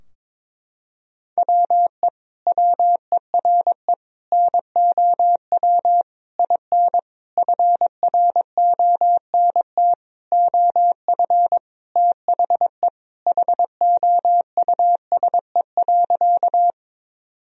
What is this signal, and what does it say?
Morse code, heard as a keyed tone, about 22 wpm, 700 Hz. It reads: WE WERE NOW IN FRONT OF THE HOUSE.
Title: Sentences from Top 1000 Words - Review Part 2 - 22wpm